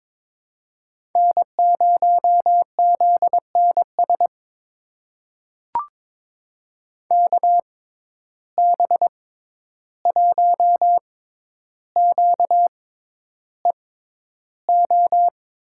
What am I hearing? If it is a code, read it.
Morse code, heard as a keyed tone, about 22 wpm, spaced as for 11 wpm, 700 Hz. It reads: N0ZNS KB1QEO
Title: USA Call Signs - 6x Character Spacing - 22wpm